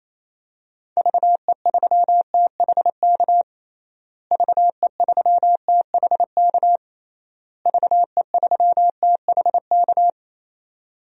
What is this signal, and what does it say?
Morse code, heard as a keyed tone, about 28 wpm, 700 Hz. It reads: VE3THK VE3THK VE3THK
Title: DXCC-100 Call Signs - 6 Characters - Repeated 3x - 28wpm